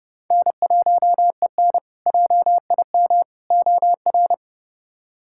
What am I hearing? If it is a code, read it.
Morse code, heard as a keyed tone, about 30 wpm, 700 Hz. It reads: N1EN JIM OR
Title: K1USN SST QSO Practice - 30wpm